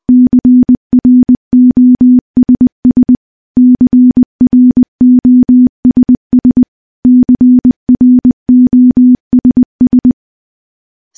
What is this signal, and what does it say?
Morse code, heard as a keyed tone, about 20 wpm, 250 Hz. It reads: CROSS CROSS CROSS
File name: morse / morse-training-20WPM